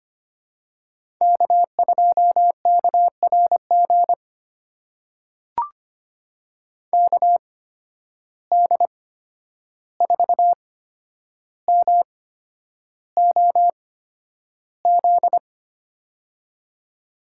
Morse code, heard as a keyed tone, about 25 wpm, spaced as for 10 wpm, 700 Hz. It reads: K2KRG KD4MOZ
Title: USA Call Signs - 8x Character Spacing - 25wpm